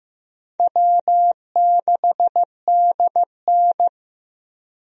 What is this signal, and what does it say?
Morse code, heard as a keyed tone, about 15 wpm, 700 Hz. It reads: W6DN
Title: USA Call Signs - 4 Characters - Repeated 2x - 15wpm